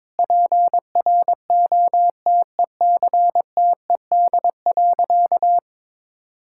Morse code, heard as a keyed tone, about 21 wpm, 700 Hz. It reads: PROTECTED.
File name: Sentences with 9-Word Phrases - 22wpm